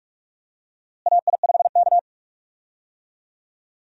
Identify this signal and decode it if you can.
Morse code, heard as a keyed tone, about 46 wpm, 700 Hz. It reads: AI5K
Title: USA Call Signs - 4 Characters - Repeated 2x - 45wpm